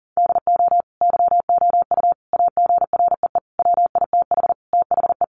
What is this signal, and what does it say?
Morse code, heard as a keyed tone, about 40 wpm, 700 Hz. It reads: DO YOU AGREE WITH THE
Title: Sentences from Top 1400 Words 40wpm